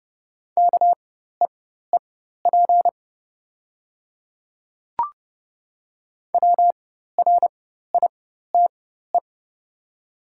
Morse code, heard as a keyed tone, about 30 wpm, spaced as for 14 wpm, 700 Hz. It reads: KEEP WRITE
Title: 4x Character Spacing - Top 500 Words - 30wpm